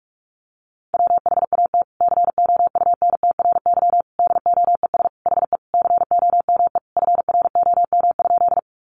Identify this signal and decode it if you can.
Morse code, heard as a keyed tone, about 44 wpm, 700 Hz. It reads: WHAT COUNTRY DOES HE COME FROM?
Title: Sentences from Top 300 Words 45wpm